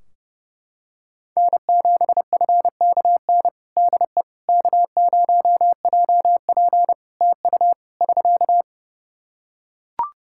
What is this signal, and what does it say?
Morse code, heard as a keyed tone, about 32 wpm, 700 Hz. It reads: N7FKN DE K0JP TU <SK>
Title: Practice QSO - Exchange 4 - Ending - 30wpm